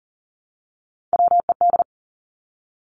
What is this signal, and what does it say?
Morse code, heard as a keyed tone, about 42 wpm, 700 Hz. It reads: WED
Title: Three Challenging Letter-Number Combinations 40wpm